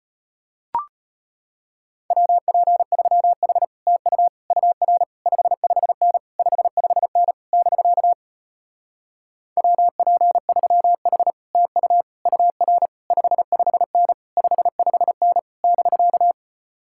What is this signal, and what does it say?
Morse code, heard as a keyed tone, about 33 wpm, 700 Hz. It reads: WP3H TU UR 55N 55N <BK> WP3H TU UR 55N 55N <BK>